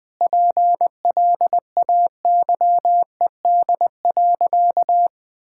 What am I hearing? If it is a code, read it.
Morse code, heard as a keyed tone, about 20 wpm, 700 Hz. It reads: PLAYED.
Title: Sentences with 6-Word Phrases - Spoken Beginning - 20wpm